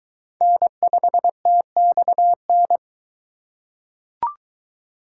Morse code, heard as a keyed tone, about 22 wpm, 700 Hz. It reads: N5TXN